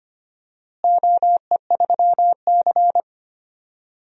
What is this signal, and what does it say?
Morse code, heard as a keyed tone, about 25 wpm, 700 Hz. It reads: OE3C
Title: DXCC-100 Call Signs - 4 Characters - Repeated 2x - 25wpm